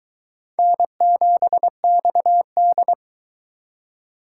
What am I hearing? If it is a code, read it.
Morse code, heard as a keyed tone, about 22 wpm, 700 Hz. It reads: N7XD